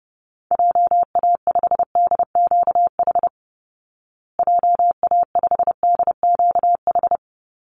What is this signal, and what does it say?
Morse code, heard as a keyed tone, about 30 wpm, 700 Hz. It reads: JA5DQH JA5DQH
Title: Call Signs Repeated 3x - 30wpm